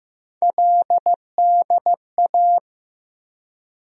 Morse code, heard as a keyed tone, about 15 wpm, 700 Hz. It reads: LDA
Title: USA Call Sign Postfixes - 3 Letters 15wpm